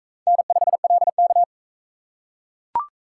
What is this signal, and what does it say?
Morse code, heard as a keyed tone, about 40 wpm, 700 Hz. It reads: N5LK